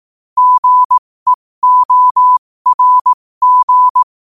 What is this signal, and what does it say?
Morse code, heard as a keyed tone, about 18 wpm, spaced as for 15 wpm, 1000 Hz. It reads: GEORG
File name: qotdfiles / 15wpm